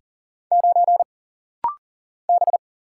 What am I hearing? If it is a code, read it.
Morse code, heard as a keyed tone, about 40 wpm, 700 Hz. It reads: 9 B
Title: Single Letter-Number - Rapid-Fire 40wpm